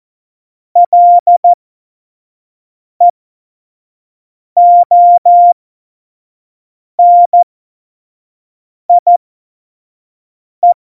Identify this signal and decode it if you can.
Morse code, heard as a keyed tone, about 14 wpm, spaced as for 5 wpm, 700 Hz. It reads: LEONIE
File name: Morsekurs / Witze / Witz 01. Dezember, 2020 25 F70